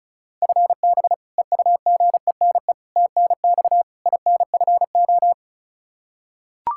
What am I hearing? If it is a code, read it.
Morse code, heard as a keyed tone, about 36 wpm, 700 Hz. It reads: FB EUGENE TNX INFO
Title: Practice QSO - Exchange 3 - More Info - 35wpm